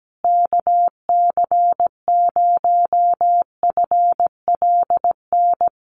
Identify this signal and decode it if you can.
Morse code, heard as a keyed tone, about 17 wpm, 700 Hz. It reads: KC0FLN
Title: USA Call Signs Repeated 3x 17wpm